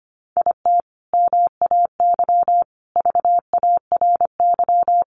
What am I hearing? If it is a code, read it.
Morse code, heard as a keyed tone, about 25 wpm, 700 Hz. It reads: IT MAY VARY
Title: Sentences from Top 1200 Words 25wpm